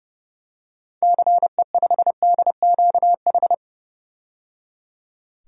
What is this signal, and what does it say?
Morse code, heard as a keyed tone, about 30 wpm, 700 Hz. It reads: CE5DQH
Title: DXCC-100 Call Signs - 6 Characters - 30wpm